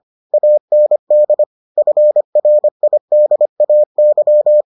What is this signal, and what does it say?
Morse code, heard as a keyed tone, about 25 wpm, 600 Hz. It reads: AND FRIDAY